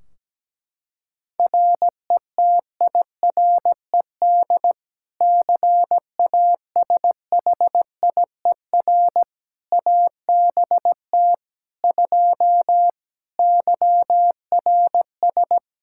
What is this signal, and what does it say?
Morse code, heard as a keyed tone, about 17 wpm, 700 Hz. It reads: RETIRED CASHIER ABT 2 YRS
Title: K1USN SST QSO Practice 17wpm